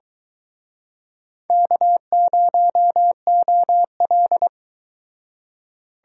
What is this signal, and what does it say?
Morse code, heard as a keyed tone, about 25 wpm, 700 Hz. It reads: K0OL